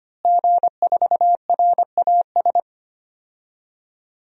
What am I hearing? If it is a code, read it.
Morse code, heard as a keyed tone, about 25 wpm, 700 Hz. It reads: G4RAS
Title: DXCC-100 Call Signs - 5 Characters - Repeated 2x - 25wpm